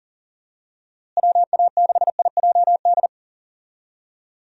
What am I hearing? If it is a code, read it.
Morse code, heard as a keyed tone, about 37 wpm, 700 Hz. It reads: WA6IJD
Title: USA Call Signs - 6 Characters - Repeated 3x - 40wpm